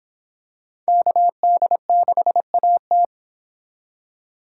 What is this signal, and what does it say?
Morse code, heard as a keyed tone, about 26 wpm, 700 Hz. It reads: KD6AT